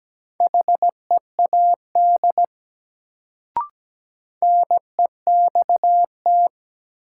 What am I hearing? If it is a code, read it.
Morse code, heard as a keyed tone, about 17 wpm, 700 Hz. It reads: HEAD NEXT
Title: Sentences from Top 300 Words - New Word Drill - 17wpm